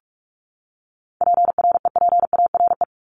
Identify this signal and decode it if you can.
Morse code, heard as a keyed tone, about 44 wpm, 700 Hz. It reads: PREPARE
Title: ED Suffix - Top 100 Words with Base Words 45wpm